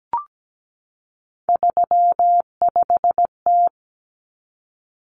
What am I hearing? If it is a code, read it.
Morse code, heard as a keyed tone, about 17 wpm, 700 Hz. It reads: 35T
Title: Common Cut Numbers - 3-Digit 17wpm